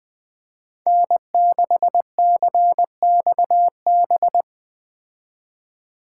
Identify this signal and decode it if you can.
Morse code, heard as a keyed tone, about 20 wpm, 700 Hz. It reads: N6CXB
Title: Speed Racing - USA 1x3 Call Signs 17wpm